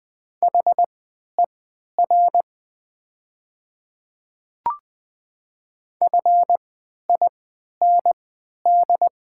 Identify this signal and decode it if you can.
Morse code, heard as a keyed tone, about 20 wpm, spaced as for 11 wpm, 700 Hz. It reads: HER FIND
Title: 3x Character Spacing - Top 100 Words - 20wpm